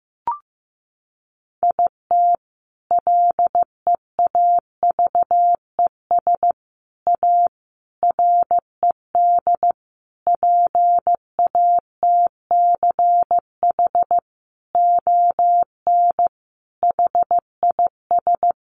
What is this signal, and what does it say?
Morse code, heard as a keyed tone, about 15 wpm, 700 Hz. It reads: IT LEAVES A RED PATCH ON HIS
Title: A Sound Of Thunder 15wpm